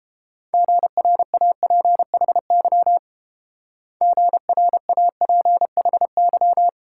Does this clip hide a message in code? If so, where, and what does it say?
Yes: Morse code from 0.5 to 6.7 s, heard as a keyed tone, about 28 wpm, 690 Hz. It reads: GRAPHY GRAPHY